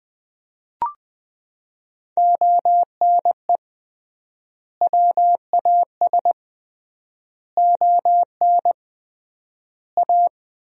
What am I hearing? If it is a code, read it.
Morse code, heard as a keyed tone, about 20 wpm, 700 Hz. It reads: ONE WAS ON A
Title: 3x Word-Spacing - Sentences from Top 300 Words - 20wpm